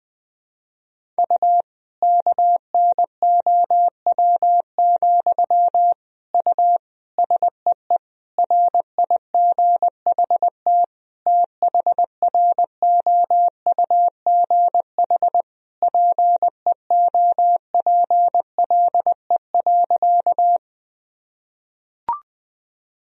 Morse code, heard as a keyed tone, about 20 wpm, 700 Hz. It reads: U KNOW, U SEE RIGHT THROUGH PEOPLE.
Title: Sentences from Top 200 Words and Abbreviations 20wpm